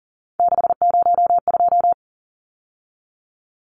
Morse code, heard as a keyed tone, about 40 wpm, 700 Hz. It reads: 602
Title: Numbers - 3-Digit 40wpm